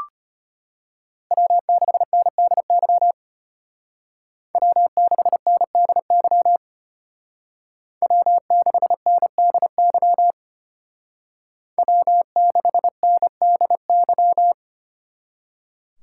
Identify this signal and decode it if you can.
Morse code, heard as a keyed tone, about 33 wpm, 700 Hz. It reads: W6NDY W6NDY W6NDY W6NDY